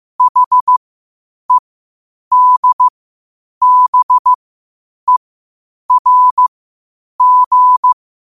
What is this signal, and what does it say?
Morse code, heard as a keyed tone, about 15 wpm, spaced as for 8 wpm, 1000 Hz. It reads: HEDBERG
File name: qotdfiles / 8wpm